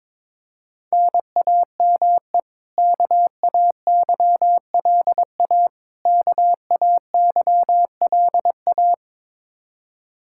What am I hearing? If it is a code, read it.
Morse code, heard as a keyed tone, about 22 wpm, 700 Hz. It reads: NAME KAYLA KAYLA
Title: Practice QSO - Exchange 1 - Beginning - 22wpm